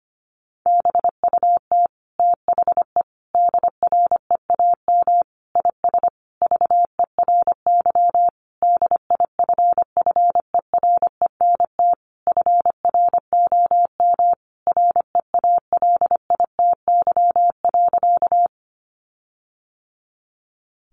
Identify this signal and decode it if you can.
Morse code, heard as a keyed tone, about 25 wpm, 700 Hz. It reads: BUT THE DREAM IS VERY DIFFERENT FROM REALITY.
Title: Sentences from Top 1500 Words 25wpm